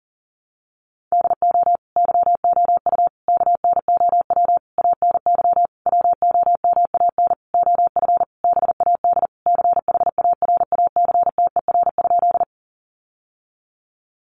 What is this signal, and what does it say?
Morse code, heard as a keyed tone, about 40 wpm, 700 Hz. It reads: DO YOU KNOW ANY WOMAN OF BAD CHARACTER?
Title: Sentences from Top 1200 Words 40wpm